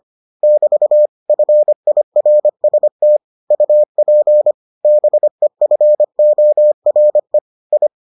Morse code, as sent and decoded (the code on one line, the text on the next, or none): -...- / ..-. .. .-. ... - / ..- .--. / -... ..-. --- .-. / ..
<BT> FIRST UP BEFORE I